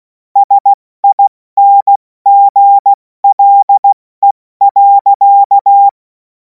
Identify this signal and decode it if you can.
Morse code, heard as a keyed tone, about 16 wpm, 800 Hz. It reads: SINGLE.
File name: Morsekurs / Witze / Witz 02. Juni 2026 70 F80